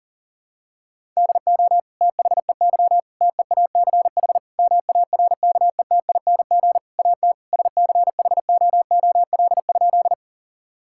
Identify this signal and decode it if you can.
Morse code, heard as a keyed tone, about 37 wpm, 700 Hz. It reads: DO THEY TEACH MARKETING AT SCHOOL?
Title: Sentences from Top 1000 Words - Review Part 5 - 40wpm